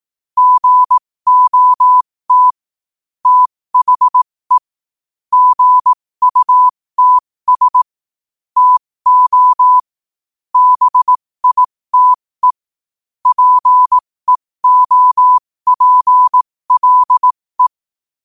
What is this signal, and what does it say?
Morse code, heard as a keyed tone, about 18 wpm, spaced as for 15 wpm, 1000 Hz. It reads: GOT THE GUTS TO BITE PEOPLE
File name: qotdfiles / 15wpm